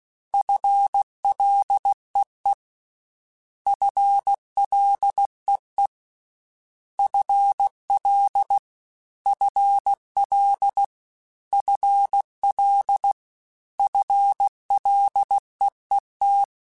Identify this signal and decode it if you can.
Morse code, heard as a keyed tone, about 16 wpm, 800 Hz. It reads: FLEE FLEE FL FL FL FLEET
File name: FL / FL mp3s / FL Prefixes 3x2 @15